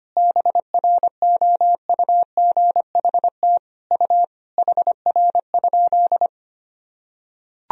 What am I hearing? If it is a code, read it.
Morse code, heard as a keyed tone, about 25 wpm, 700 Hz. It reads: BROUGHT U HR?
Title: Sentences from Top 500 Words and Abbreviations 25wpm